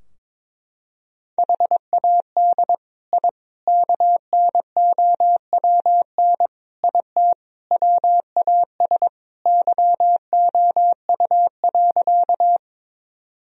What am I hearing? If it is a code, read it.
Morse code, heard as a keyed tone, about 22 wpm, 700 Hz. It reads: HAD I KNOWN IT WAS YOU.
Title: Sentences from Top 1000 Words - Review Part 3 - 22wpm